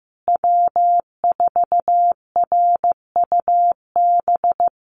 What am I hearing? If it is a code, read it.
Morse code, heard as a keyed tone, about 15 wpm, 700 Hz. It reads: W4RUB
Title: USA Call Signs Repeated 2x 15wpm